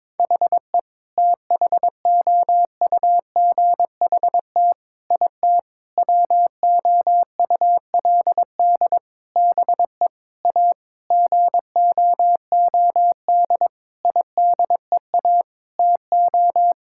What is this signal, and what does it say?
Morse code, heard as a keyed tone, about 22 wpm, 700 Hz. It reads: HE THOUGHT IT WOULD BE A GOOD IDEA TO
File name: practice / 22wpm / Sentences with 8-Word Phrases - 22wpm